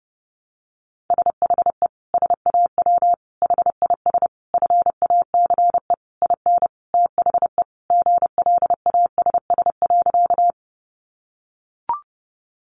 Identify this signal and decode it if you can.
Morse code, heard as a keyed tone, about 30 wpm, 700 Hz. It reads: SHE SAW HIS FACE IN THE GLASS.
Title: Sentences from Top 1300 Words 30wpm